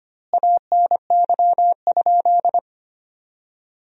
Morse code, heard as a keyed tone, about 25 wpm, 700 Hz. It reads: ANY?
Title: 2x Word-Spacing - Sentences from Top 300 Words - 25wpm